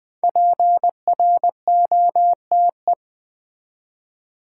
Morse code, heard as a keyed tone, about 20 wpm, 700 Hz. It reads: PROTE
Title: Build A Word - 5 Letters At A Time - 10-Letter Words - 20wpm